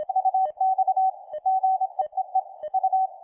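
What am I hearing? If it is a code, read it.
Morse code, heard as a keyed tone, about 26 wpm, 760 Hz, with radio noise. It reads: UX GEEE U
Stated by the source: Full QSK operation with a PFR-3A QRP transceiver, tested on a weak station's signal, Velko, interrupting his CQ.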